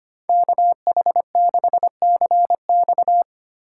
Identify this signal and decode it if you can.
Morse code, heard as a keyed tone, about 30 wpm, 700 Hz. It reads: KH6CX
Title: Speed Racing - USA 2x2 Call Signs 25wpm